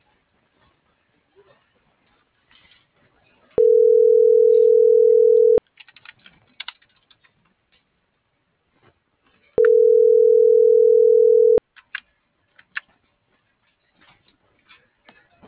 Ringback tone (North America), 440 + 480 Hz.